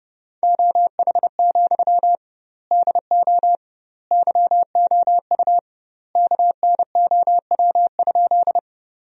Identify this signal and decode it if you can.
Morse code, heard as a keyed tone, about 30 wpm, 700 Hz. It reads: OH, DO YOU KNOW?
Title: 2x Word-Spacing - Sentences from Top 500 Words - 30wpm